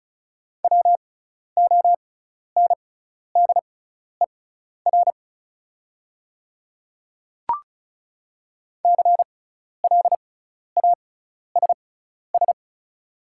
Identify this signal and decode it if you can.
Morse code, heard as a keyed tone, about 34 wpm, spaced as for 12 wpm, 700 Hz. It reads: WONDER CLASS
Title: 6x Character Spacing - Top 500 Words - 35wpm